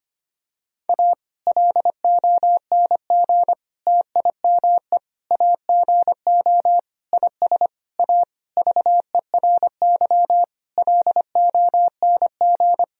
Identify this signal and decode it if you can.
Morse code, heard as a keyed tone, about 25 wpm, 700 Hz. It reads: A LONG TIME AGO IS A VERY LONG
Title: Sentences from Top 1000 Words - Review Part 3 - 25wpm